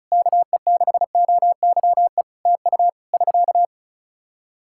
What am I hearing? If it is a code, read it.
Morse code, heard as a keyed tone, about 34 wpm, 700 Hz. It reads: KE6OYE TU <SK>
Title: Practice QSO - Exchange 4 - Ending - 35wpm